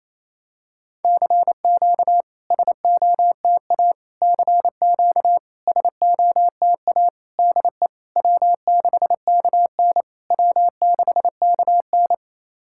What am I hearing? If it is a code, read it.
Morse code, heard as a keyed tone, about 28 wpm, 690 Hz. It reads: CQ SOTA CQ SOTA DE W6KN W6KN
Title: SOTA QSO Practice 28wpm